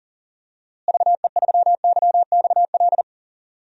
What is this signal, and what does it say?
Morse code, heard as a keyed tone, about 40 wpm, 700 Hz. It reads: VE3YXL
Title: DXCC-100 Call Signs - 6 Characters - Repeated 2x - 40wpm